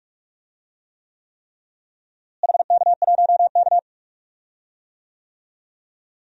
Morse code, heard as a keyed tone, about 46 wpm, 700 Hz. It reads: HK1K
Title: DXCC-100 Call Signs - 4 Characters - Repeated 3x - 45wpm